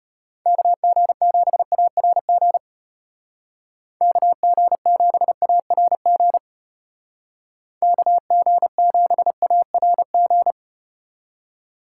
Morse code, heard as a keyed tone, about 34 wpm, 700 Hz. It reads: KG7ARG KG7ARG KG7ARG